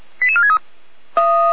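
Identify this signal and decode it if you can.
Five-tone selective call (ZVEI), heard as quick runs of five tones, tones of about 70 ms. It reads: 88463
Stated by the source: ZVEI five-tone selective call received by SDR.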